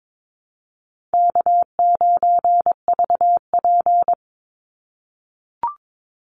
Morse code, heard as a keyed tone, about 22 wpm, 700 Hz. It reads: K9VP
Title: Call Signs - 22wpm